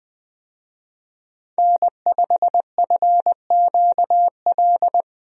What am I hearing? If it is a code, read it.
Morse code, heard as a keyed tone, about 20 wpm, 700 Hz. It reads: N5FQL
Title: USA Call Signs - 5 Characters - Repeated 2x - 20wpm